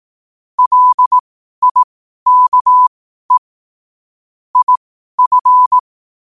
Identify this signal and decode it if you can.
Morse code, heard as a keyed tone, about 18 wpm, spaced as for 12 wpm, 1000 Hz. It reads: LIKE IF
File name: qotdfiles / 12wpm